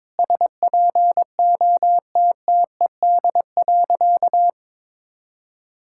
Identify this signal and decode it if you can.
Morse code, heard as a keyed tone, about 22 wpm, 700 Hz. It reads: SPOTTED.
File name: All-Age Appropriate Jokes - 22wpm